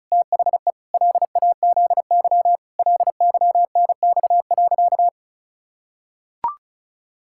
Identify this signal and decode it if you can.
Morse code, heard as a keyed tone, about 36 wpm, 700 Hz. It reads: THE LAZY LYNX.
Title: Pangram Challenge v2 - 35wpm